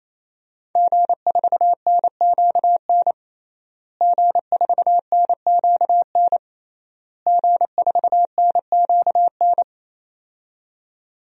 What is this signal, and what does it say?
Morse code, heard as a keyed tone, about 27 wpm, 700 Hz. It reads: G4NQN G4NQN G4NQN
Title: DXCC-100 Call Signs - 5 Characters - Repeated 3x - 28wpm